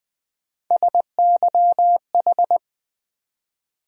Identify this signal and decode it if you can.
Morse code, heard as a keyed tone, about 20 wpm, 700 Hz. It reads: SYH
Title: USA Call Sign Postfixes - 3 Letters 20wpm